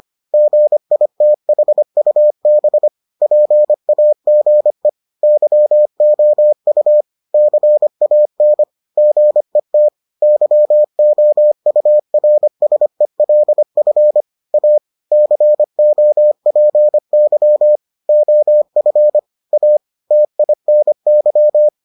Morse code, heard as a keyed tone, about 25 wpm, 600 Hz. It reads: GITHUB PAGE YOU CAN GET YOURSELF A COPY OF A TINY